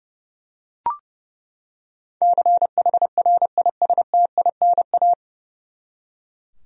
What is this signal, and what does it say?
Morse code, heard as a keyed tone, about 30 wpm, 700 Hz. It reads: CHRISTINA
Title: USA Common Names - 30wpm